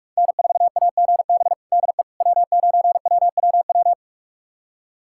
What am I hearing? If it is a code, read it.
Morse code, heard as a keyed tone, about 41 wpm, 700 Hz. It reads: N4AGB DE W9WWW
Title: USA Callsign DE Callsign - 5 Characters - 45wpm